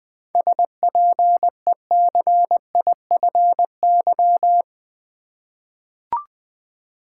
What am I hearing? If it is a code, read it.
Morse code, heard as a keyed tone, about 20 wpm, 700 Hz. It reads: SPECIFY
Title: Seven Letter Words - 20wpm